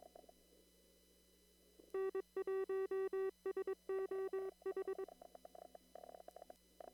Morse code, heard as a keyed tone, about 22 wpm, 380 Hz. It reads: N1SOH
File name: CW / kt5j wpx cw-op-k5pi